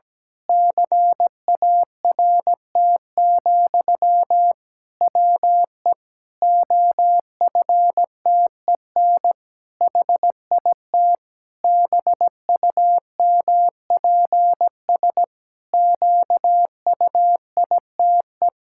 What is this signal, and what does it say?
Morse code, heard as a keyed tone, about 17 wpm, 700 Hz. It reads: CART, WE OFTEN HIT BUMPS QUITE
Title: Pangram Challenge v2 - 17wpm